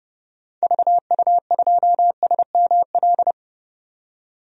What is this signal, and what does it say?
Morse code, heard as a keyed tone, about 31 wpm, 700 Hz. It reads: VU2SML
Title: DXCC-100 Call Signs - 6 Characters - 30wpm